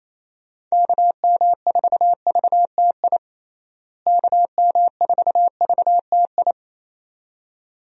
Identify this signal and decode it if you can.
Morse code, heard as a keyed tone, about 28 wpm, 700 Hz. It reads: KM4VTI KM4VTI
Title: USA Call Signs - 6 Characters - Repeated 2x - 28wpm